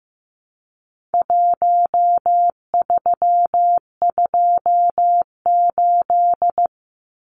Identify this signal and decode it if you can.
Morse code, heard as a keyed tone, about 15 wpm, 700 Hz. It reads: 1328
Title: Numbers - Contest Style - 4-Digit 15wpm